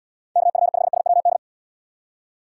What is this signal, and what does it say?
Morse code, heard as a keyed tone, about 70 wpm, 700 Hz. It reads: KF5SWD